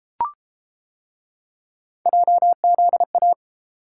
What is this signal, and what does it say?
Morse code, heard as a keyed tone, about 30 wpm, 700 Hz. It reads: JZA